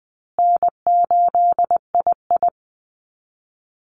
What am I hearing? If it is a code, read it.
Morse code, heard as a keyed tone, about 20 wpm, 700 Hz. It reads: N8II
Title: Easy Call Signs Repeated 2x - 20wpm